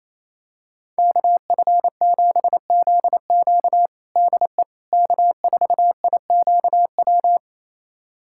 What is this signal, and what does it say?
Morse code, heard as a keyed tone, about 27 wpm, 690 Hz. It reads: KF7ZQ DE K4IQW
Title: USA Callsign DE Callsign - 5 Characters - 28wpm